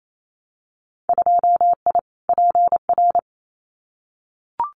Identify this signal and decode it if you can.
Morse code, heard as a keyed tone, about 28 wpm, 700 Hz. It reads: 2I PR
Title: Winter Field Day Exchanges Encore 28wpm